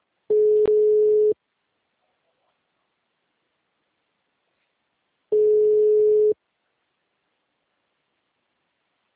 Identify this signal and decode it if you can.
Ringback tone (Europe, 425 Hz).
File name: out- -2051- - - .19085